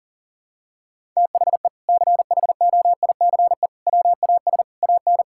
Morse code, heard as a keyed tone, about 40 wpm, 700 Hz. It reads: THE CHOICE WAS AN
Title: Sentences from Top 1000 Words - Review Part 6 - 40wpm